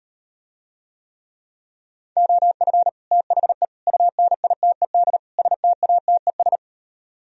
Morse code, heard as a keyed tone, about 39 wpm, 700 Hz. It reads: OF THE UNITED STATES